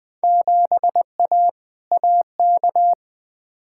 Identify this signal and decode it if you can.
Morse code, heard as a keyed tone, about 21 wpm, 700 Hz. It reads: 7A AK
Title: Field Day Exchanges Encore - 20wpm